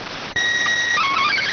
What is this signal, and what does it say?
SSTV (slow-scan television), PD 290.